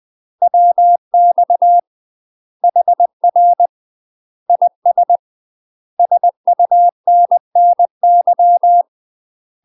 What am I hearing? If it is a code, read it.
Morse code, heard as a keyed tone, about 20 wpm, 700 Hz. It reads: WX HR IS SUNNY
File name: UK MASTER QSOs TRAINING/20WPM